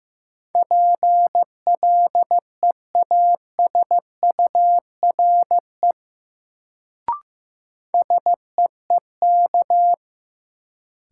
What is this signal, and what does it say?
Morse code, heard as a keyed tone, about 15 wpm, 700 Hz. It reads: PLEASURE SEEK
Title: Sentences from Top 1600 Words - New Word Drill - 15wpm